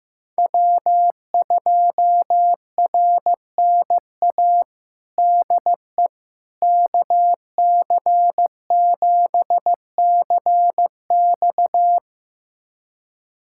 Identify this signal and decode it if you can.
Morse code, heard as a keyed tone, about 16 wpm, 700 Hz. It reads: W2RNA DE KC7CX
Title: USA Callsign DE Callsign - 5 Characters - 15wpm